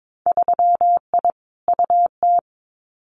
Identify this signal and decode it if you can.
Morse code, heard as a keyed tone, about 22 wpm, 700 Hz. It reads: 3I UT
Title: Winter Field Day Exchanges Encore 22wpm